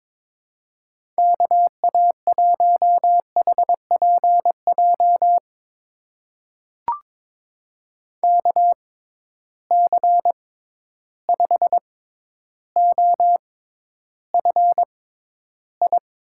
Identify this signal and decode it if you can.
Morse code, heard as a keyed tone, about 22 wpm, spaced as for 11 wpm, 700 Hz. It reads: KA1HPJ KC5OFI
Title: USA Call Signs - 6 Characters - 6x Character Spacing - 22wpm